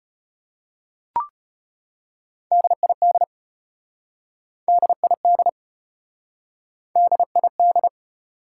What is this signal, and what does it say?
Morse code, heard as a keyed tone, about 34 wpm, 700 Hz. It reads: DID DID DID